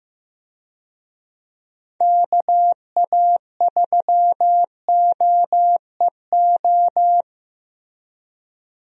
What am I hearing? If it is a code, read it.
Morse code, heard as a keyed tone, about 15 wpm, 700 Hz. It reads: KA3OEO
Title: USA Call Signs - 6 Characters - Repeated 2x - 15wpm